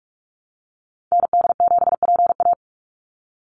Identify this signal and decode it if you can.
Morse code, heard as a keyed tone, about 44 wpm, 700 Hz. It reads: ND7PA
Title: NA SOTA Activator Callsigns 45wpm